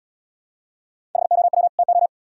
Morse code, heard as a keyed tone, about 80 wpm, 700 Hz. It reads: HOW TO